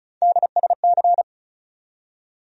Morse code, heard as a keyed tone, about 36 wpm, 700 Hz. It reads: DSC
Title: USA Call Sign Postfixes - 3 Letters 35wpm